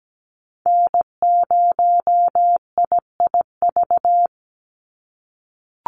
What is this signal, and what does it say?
Morse code, heard as a keyed tone, about 17 wpm, 700 Hz. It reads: N0IIV
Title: USA Call Signs 17wpm